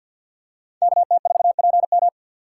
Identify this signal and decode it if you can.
Morse code, heard as a keyed tone, about 52 wpm, 700 Hz. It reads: KT4PM
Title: USA Call Signs - 5 Characters - 3x Character Spacing - 50wpm